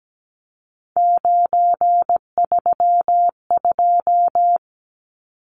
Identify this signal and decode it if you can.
Morse code, heard as a keyed tone, about 17 wpm, 700 Hz. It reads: 932
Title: Numbers - 3-Digit 17wpm